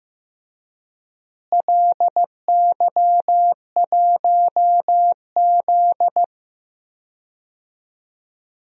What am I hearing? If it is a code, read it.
Morse code, heard as a keyed tone, about 15 wpm, 700 Hz. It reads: LY1Z
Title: DXCC-100 Call Signs - 4 Characters - Repeated 2x - 15wpm